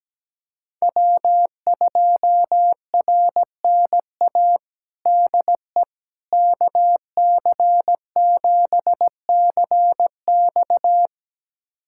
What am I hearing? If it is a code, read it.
Morse code, heard as a keyed tone, about 17 wpm, 700 Hz. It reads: W2RNA DE KC7CX
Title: USA Callsign DE Callsign - 5 Characters - 17wpm